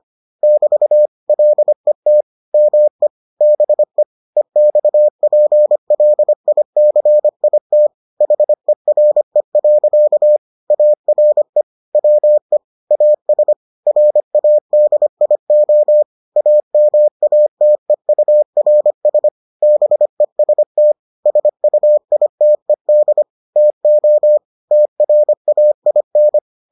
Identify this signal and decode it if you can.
Morse code, heard as a keyed tone, about 25 wpm, 600 Hz. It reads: <BT> LET ME BE EXPLICIT HERE. ARE WE AS RADIO AMATEURS BEST SUITED TO TRAIN